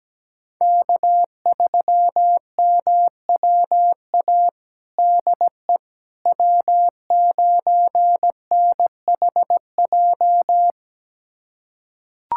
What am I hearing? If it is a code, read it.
Morse code, heard as a keyed tone, about 17 wpm, 700 Hz. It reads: K3MWA DE W9NHJ
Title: USA Callsign DE Callsign - 5 Characters - 17wpm